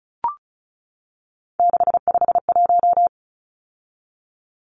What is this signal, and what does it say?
Morse code, heard as a keyed tone, about 34 wpm, 700 Hz. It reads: 651
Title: Numbers - 3-Digit 35wpm